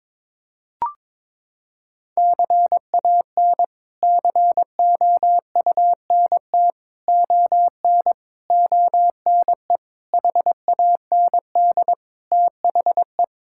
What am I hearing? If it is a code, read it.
Morse code, heard as a keyed tone, about 22 wpm, 700 Hz. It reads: CAN COUNT ON ONE HAND THE
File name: practice / 22wpm / Sets of 8 Words - 22wpm